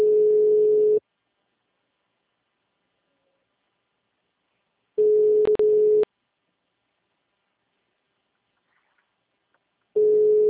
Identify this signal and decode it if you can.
Ringback tone (Europe, 425 Hz).
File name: out- -2004- - - .40529